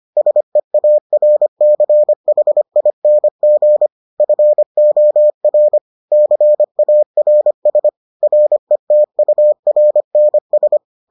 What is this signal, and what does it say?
Morse code, heard as a keyed tone, about 25 wpm, 600 Hz. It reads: SEARCHING FOR CARS RETURNS